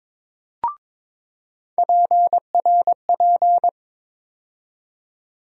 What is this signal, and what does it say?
Morse code, heard as a keyed tone, about 22 wpm, 700 Hz. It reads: PRP
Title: USA Call Sign Postfixes - 3 Letters 22wpm